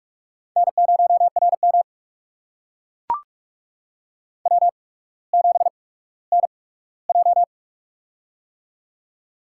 Morse code, heard as a keyed tone, about 42 wpm, spaced as for 19 wpm, 700 Hz. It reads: N0RM W7NJ
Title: USA Call Signs - 4 Characters - 8x Character Spacing - 45wpm